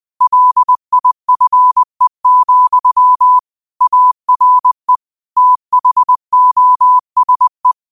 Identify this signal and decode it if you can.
Morse code, heard as a keyed tone, about 20 wpm, 1000 Hz. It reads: LIFE, ARE THOSE
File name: qotdfiles / 20wpm